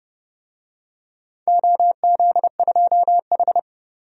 Morse code, heard as a keyed tone, about 30 wpm, 700 Hz. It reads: OZ2H
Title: DXCC-100 Call Signs - 4 Characters - 30wpm